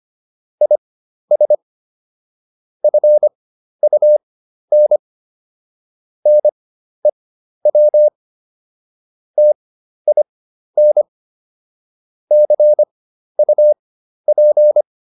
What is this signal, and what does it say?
Morse code, heard as a keyed tone, about 25 wpm, spaced as for 12 wpm, 600 Hz. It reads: IS FUN NEW TIN CUP